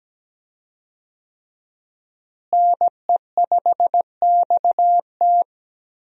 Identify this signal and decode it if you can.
Morse code, heard as a keyed tone, about 17 wpm, 700 Hz. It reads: NE5XT
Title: USA Call Signs - 5 Characters - Repeated 3x - 17wpm